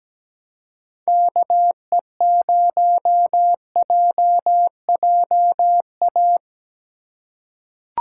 Morse code, heard as a keyed tone, about 17 wpm, 700 Hz. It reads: KE0JJA
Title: USA Call Signs - 2x Character Spacing - 17wpm